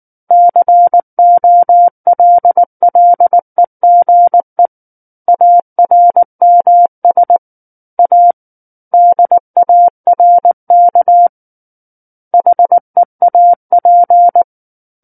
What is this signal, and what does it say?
Morse code, heard as a keyed tone, about 19 wpm, 700 Hz. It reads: COLLEGE ARMS A DARK HEAP
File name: Book One / 10-Chapter 10 - 19 WPM